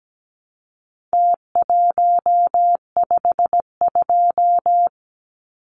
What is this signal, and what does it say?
Morse code, heard as a keyed tone, about 17 wpm, 700 Hz. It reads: T152
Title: Numbers - Contest Style - 4-Digit 17wpm